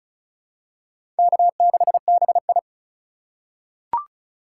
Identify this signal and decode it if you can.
Morse code, heard as a keyed tone, about 34 wpm, 700 Hz. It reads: K6BI
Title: USA Call Signs - 4 Characters - 2x Character Spacing - 35wpm